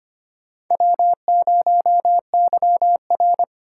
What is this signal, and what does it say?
Morse code, heard as a keyed tone, about 25 wpm, 700 Hz. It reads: W0YR
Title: K1USN SST QSO Practice - 25wpm